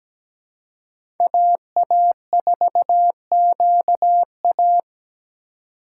Morse code, heard as a keyed tone, about 17 wpm, 700 Hz. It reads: AA4QA
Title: USA Call Signs - 5 Characters - 4x Character Spacing - 17wpm